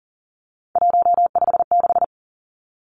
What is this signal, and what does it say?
Morse code, heard as a keyed tone, about 39 wpm, 700 Hz. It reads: 156
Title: Numbers - 3-Digit 40wpm